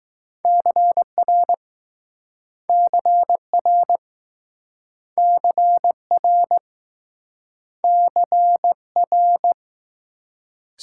Morse code, heard as a keyed tone, about 21 wpm, 700 Hz. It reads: CR CR CR CR